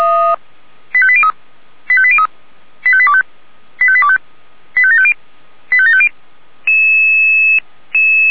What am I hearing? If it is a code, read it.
Five-tone selective call (ZVEI), heard as quick runs of five tones, tones of about 70 ms. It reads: 85803 85803 85825 85825 85750 85750